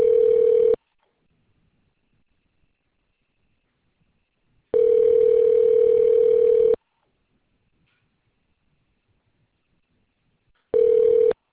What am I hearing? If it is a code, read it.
Ringback tone (North America), 440 + 480 Hz.